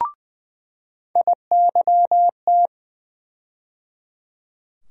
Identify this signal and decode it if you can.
Morse code, heard as a keyed tone, about 20 wpm, 700 Hz. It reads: IYT
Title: USA Call Sign Postfixes - 3 Letters 20wpm